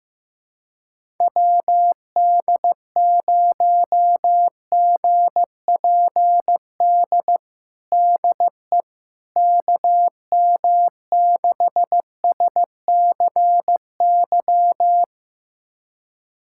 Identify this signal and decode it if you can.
Morse code, heard as a keyed tone, about 15 wpm, 700 Hz. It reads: WD0GPD DE KM6SCY
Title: USA Callsign DE Callsign 15wpm